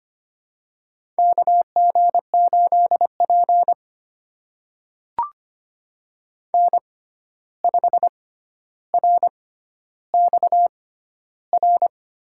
Morse code, heard as a keyed tone, about 25 wpm, spaced as for 12 wpm, 700 Hz. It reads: KG8P N5RXR
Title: USA Call Signs - 6x Character Spacing - 25wpm